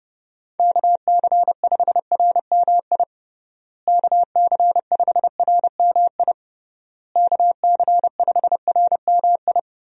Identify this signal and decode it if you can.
Morse code, heard as a keyed tone, about 30 wpm, 700 Hz. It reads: KC5RMI KC5RMI KC5RMI
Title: USA Call Signs - 6 Characters - Repeated 3x - 30wpm